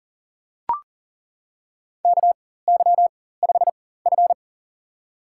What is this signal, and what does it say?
Morse code, heard as a keyed tone, about 40 wpm, spaced as for 19 wpm, 700 Hz. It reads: KY5F
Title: USA Call Signs - 4 Characters - 4x Character Spacing - 40wpm